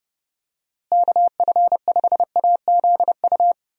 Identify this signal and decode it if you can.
Morse code, heard as a keyed tone, about 30 wpm, 700 Hz. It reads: KF5AZU
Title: USA Call Signs - 6 Characters - Repeated 2x - 30wpm